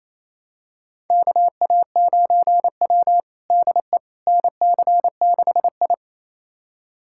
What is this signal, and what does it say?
Morse code, heard as a keyed tone, about 28 wpm, 700 Hz. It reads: KA9W DE NC6I
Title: USA Callsign DE Callsign - 4 Characters - 28wpm